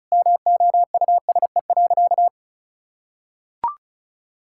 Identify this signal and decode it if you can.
Morse code, heard as a keyed tone, about 34 wpm, 700 Hz. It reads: MOUSE.
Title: Sentences from Top 1000 Words - Review Part 5 - 35wpm